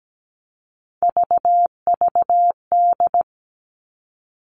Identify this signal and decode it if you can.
Morse code, heard as a keyed tone, about 17 wpm, 700 Hz. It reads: VVD
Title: Three Challenging Letter-Number Combinations 17wpm